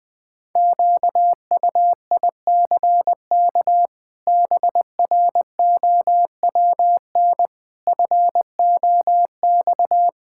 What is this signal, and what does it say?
Morse code, heard as a keyed tone, about 20 wpm, 700 Hz. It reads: QUICK BROWN FOX
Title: Sets of 8 Words - 20wpm